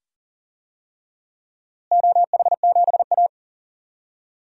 Morse code, heard as a keyed tone, about 37 wpm, 700 Hz. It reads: OH7A
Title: DXCC-100 Call Signs - 4 Characters - Repeated 2x - 40wpm